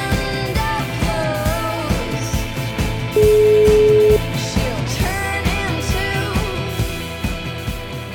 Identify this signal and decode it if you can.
Ringback tone (Europe, 425 Hz), with noise.